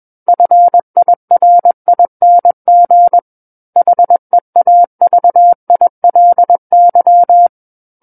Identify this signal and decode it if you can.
Morse code, heard as a keyed tone, about 21 wpm, 700 Hz. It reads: FIRING HEAVILY
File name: Book One / 12-Chapter 12 - 21 WPM